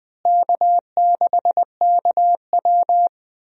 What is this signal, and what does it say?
Morse code, heard as a keyed tone, about 20 wpm, 700 Hz. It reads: K6KW
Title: USA Call Signs - 4 Characters - Repeated 2x - 20wpm